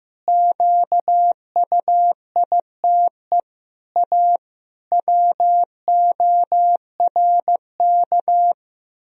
Morse code, heard as a keyed tone, about 16 wpm, 700 Hz. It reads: QUITE A WORK
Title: Sentences from Top 1000 Words - Review Part 3 - 15wpm